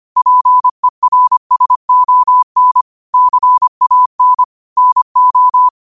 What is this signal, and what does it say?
Morse code, heard as a keyed tone, about 25 wpm, 1000 Hz. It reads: PERSON CAN NO
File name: qotdfiles / 25wpm